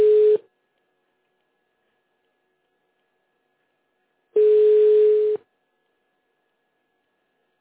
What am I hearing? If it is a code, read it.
Ringback tone (Europe, 425 Hz).
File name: 1015. -d785-4ec4-a828